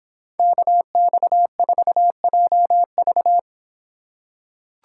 Morse code, heard as a keyed tone, about 26 wpm, 700 Hz. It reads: KX4JV